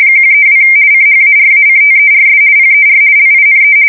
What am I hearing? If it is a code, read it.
RTTY, about 50 baud. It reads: 132 CHARACTERS PER LINE
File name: fsk170-50B / teletype-corp